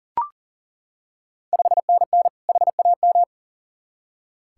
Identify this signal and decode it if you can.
Morse code, heard as a keyed tone, about 38 wpm, 700 Hz. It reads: 5NN HAM
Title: Ontario QSO Party Exchange 40wpm